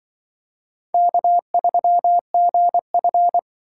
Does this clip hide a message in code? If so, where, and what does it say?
Yes: Morse code from 0.9 to 3.4 s, heard as a keyed tone, about 25 wpm, 700 Hz. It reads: K3GF